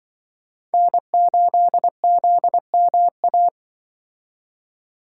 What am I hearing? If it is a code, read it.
Morse code, heard as a keyed tone, about 24 wpm, 700 Hz. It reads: N8ZMA
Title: Speed Racing - USA 1x3 Call Signs 20wpm